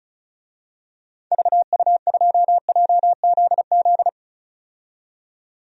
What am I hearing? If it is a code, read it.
Morse code, heard as a keyed tone, about 35 wpm, 700 Hz. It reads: VU2JZZ
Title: DXCC-100 Call Signs - 6 Characters - 35wpm